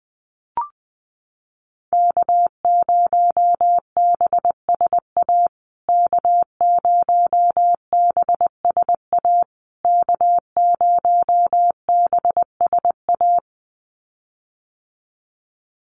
Morse code, heard as a keyed tone, about 20 wpm, 700 Hz. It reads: K0BSA K0BSA K0BSA
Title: USA Call Signs Repeated 3x 20wpm